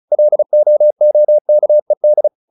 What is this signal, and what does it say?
Morse code, heard as a keyed tone, about 34 wpm, 600 Hz. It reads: LOOKED
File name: the old man and the sea 2 35wpm